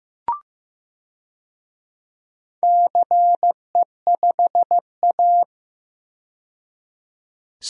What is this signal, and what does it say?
Morse code, heard as a keyed tone, about 16 wpm, 700 Hz. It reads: CE5A
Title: DXCC-100 Call Signs - 4 Characters - 15wpm